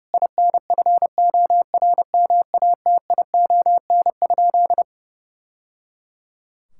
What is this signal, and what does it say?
Morse code, heard as a keyed tone, about 30 wpm, 700 Hz. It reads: INFORMATION?
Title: Sentences from Top 1000 Words - Review Part 6 - 30wpm